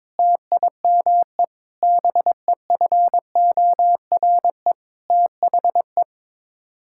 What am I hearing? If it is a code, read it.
Morse code, heard as a keyed tone, about 21 wpm, 700 Hz. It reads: TIME BEFORE THE
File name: Sets of 9 Words - 22wpm